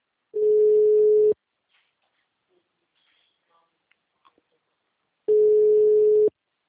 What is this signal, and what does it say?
Ringback tone (Europe, 425 Hz).
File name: out- -2002- - - .31231